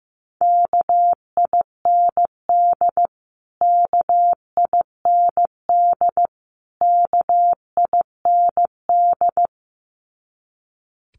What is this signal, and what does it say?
Morse code, heard as a keyed tone, about 15 wpm, 700 Hz. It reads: KIND KIND KIND
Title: Top 500 Words Repeated 15wpm